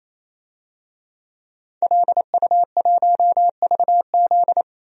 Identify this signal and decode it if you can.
Morse code, heard as a keyed tone, about 28 wpm, 700 Hz. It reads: LU1VZ
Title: DXCC-100 Call Signs - 5 Characters - 28wpm